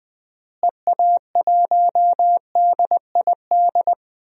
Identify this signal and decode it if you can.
Morse code, heard as a keyed tone, about 20 wpm, 700 Hz. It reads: EA1DID
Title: DXCC-100 Call Signs - 6 Characters - Repeated 2x - 20wpm